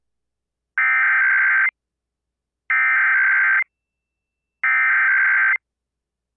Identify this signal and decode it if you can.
Emergency Alert System (SAME) tones (header bursts), heard as harsh buzzing bursts. It is heard from 0.8 to 5.4 s.